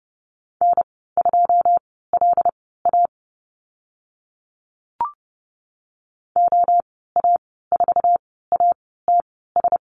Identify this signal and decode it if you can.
Morse code, heard as a keyed tone, about 30 wpm, spaced as for 17 wpm, 700 Hz. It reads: N2LA OA4ATS
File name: Lesson 15g - Number 5 - Call Signs 30wpm